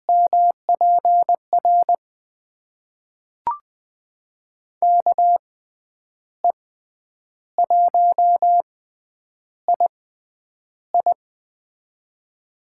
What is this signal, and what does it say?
Morse code, heard as a keyed tone, about 20 wpm, spaced as for 9 wpm, 700 Hz. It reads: MPR KE1II
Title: USA Call Signs - 6x Character Spacing - 20wpm